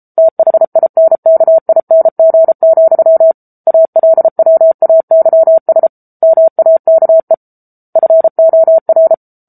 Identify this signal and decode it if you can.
Morse code, heard as a keyed tone, about 34 wpm, 650 Hz. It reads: THINKING, ALWAYS MAKE FOR
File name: Book Two / 07-Chapter 7 - 33 WPM